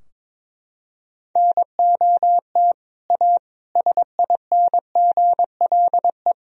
Morse code, heard as a keyed tone, about 22 wpm, 700 Hz. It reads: NOT A SINGLE
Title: Sentences from Top 1000 Words - Review Part 4 - 22wpm